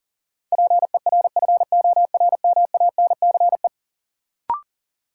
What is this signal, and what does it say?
Morse code, heard as a keyed tone, about 40 wpm, 700 Hz. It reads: PERFORMANCE
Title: Sentences from Top 1000 Words - New Word Drill - 40wpm